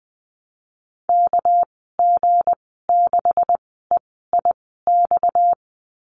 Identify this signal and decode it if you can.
Morse code, heard as a keyed tone, about 20 wpm, spaced as for 15 wpm, 700 Hz. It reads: KG6EIX
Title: 238 Lesson 38g - Letter q - Call Signs 20wpm